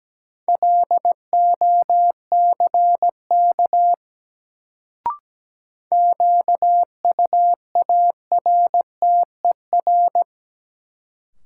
Morse code, heard as a keyed tone, about 17 wpm, 700 Hz. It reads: LOCK QUARTER
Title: Sentences from Top 1300 Words - New Word Drill Alt - 17wpm